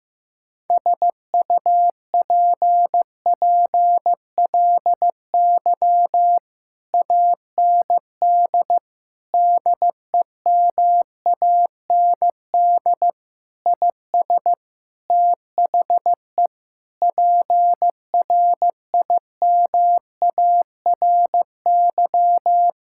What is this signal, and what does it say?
Morse code, heard as a keyed tone, about 15 wpm, 700 Hz. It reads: SUPPLY AND DEMAND IS THE PRIMARY
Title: Sentences from Top 1000 Words - Review Part 6 - 15wpm